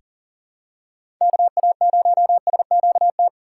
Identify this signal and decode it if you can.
Morse code, heard as a keyed tone, about 40 wpm, 700 Hz. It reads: KA0SQT
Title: USA Call Signs - 6 Characters - 2x Character Spacing - 40wpm